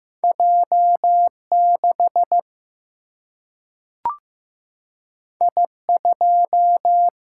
Morse code, heard as a keyed tone, about 15 wpm, 700 Hz. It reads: J 6 I 2
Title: Single-Letter Single-Number 15wpm